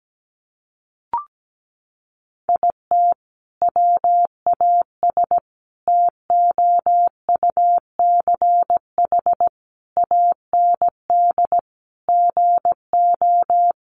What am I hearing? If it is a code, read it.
Morse code, heard as a keyed tone, about 17 wpm, 700 Hz. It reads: IT WAS TOUCH AND GO
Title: Sentences from Top 1200 Words 17wpm